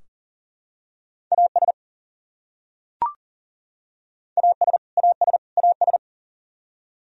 Morse code, heard as a keyed tone, about 40 wpm, 700 Hz. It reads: AS AS AS AS
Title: Top 100 Words Repeated 40wpm